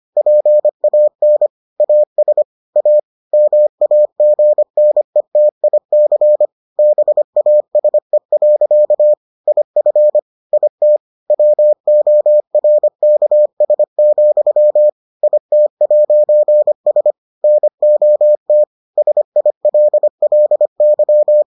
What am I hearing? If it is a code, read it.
Morse code, heard as a keyed tone, about 25 wpm, 600 Hz. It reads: PAN AS A MAGNETIC BASE. IF IT WORKS, IT'S NOT SILLY